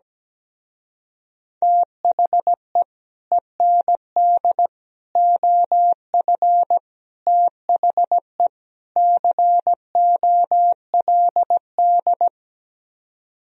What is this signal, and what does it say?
Morse code, heard as a keyed tone, about 17 wpm, 700 Hz. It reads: THE END OF THE COLD
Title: Speed-Racing - Sets of 5 Words 17wpm